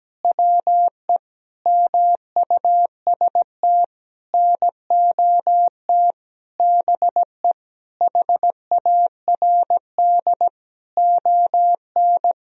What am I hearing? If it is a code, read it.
Morse code, heard as a keyed tone, about 17 wpm, 700 Hz. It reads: WE MUST NOT BE HARD ON
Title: Sentences from Top 1000 Words - Review Part 1 - 17wpm